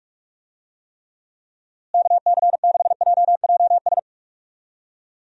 Morse code, heard as a keyed tone, about 42 wpm, 700 Hz. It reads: KC6JJS
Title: USA Call Signs - 6 Characters - Repeated 2x - 45wpm